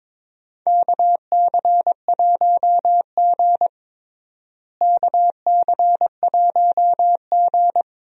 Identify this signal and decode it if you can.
Morse code, heard as a keyed tone, about 22 wpm, 700 Hz. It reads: KC1G KC1G
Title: USA Call Signs - 4 Characters - Repeated 3x - 22wpm